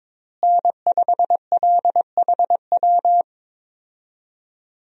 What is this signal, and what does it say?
Morse code, heard as a keyed tone, about 22 wpm, 700 Hz. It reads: N5LHW
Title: USA Call Signs - 5 Characters - Repeated 3x - 22wpm